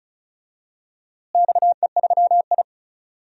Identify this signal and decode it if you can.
Morse code, heard as a keyed tone, about 36 wpm, 700 Hz. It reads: XE3I
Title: DXCC-100 Call Signs - 4 Characters - Repeated 2x - 35wpm